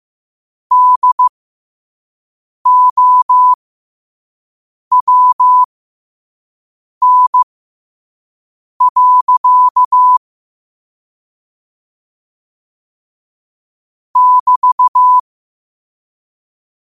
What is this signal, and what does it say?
Morse code, heard as a keyed tone, about 15 wpm, spaced as for 5 wpm, 1000 Hz. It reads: DOWN. <BT>
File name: qotdfiles / 5wpm